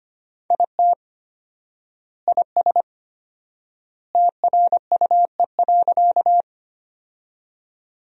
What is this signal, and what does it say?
Morse code, heard as a keyed tone, about 25 wpm, 700 Hz. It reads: IT IS TRUE.
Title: 4x Word-Spacing - Sentences from Top 400 Words - 25wpm